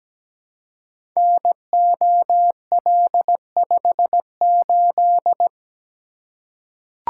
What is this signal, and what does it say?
Morse code, heard as a keyed tone, about 17 wpm, 700 Hz. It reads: NOL58
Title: Code Groups - 5 Letter:Numbers - 17wpm